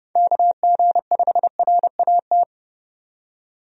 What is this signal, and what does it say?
Morse code, heard as a keyed tone, about 32 wpm, 700 Hz. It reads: KG5RAT
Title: USA Call Signs - 6 Characters - 30wpm